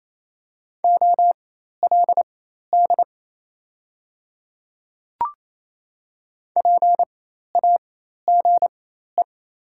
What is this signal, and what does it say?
Morse code, heard as a keyed tone, about 27 wpm, spaced as for 13 wpm, 690 Hz. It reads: OLD PAGE
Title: 4x Character Spacing - Top 300 Words - 28wpm